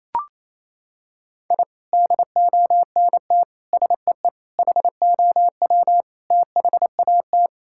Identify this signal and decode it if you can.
Morse code, heard as a keyed tone, about 27 wpm, 700 Hz. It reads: I DONT SEE HOW THAT
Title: Sentences from Top 1000 Words - Review Part 6 - 28wpm